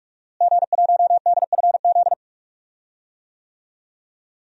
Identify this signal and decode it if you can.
Morse code, heard as a keyed tone, about 41 wpm, 700 Hz. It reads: G1DFZ
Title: DXCC-100 Call Signs - 5 Characters - Repeated 3x - 45wpm